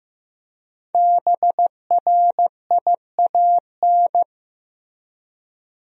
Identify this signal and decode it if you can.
Morse code, heard as a keyed tone, about 15 wpm, 700 Hz. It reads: BRIAN
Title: USA Common Names - 15wpm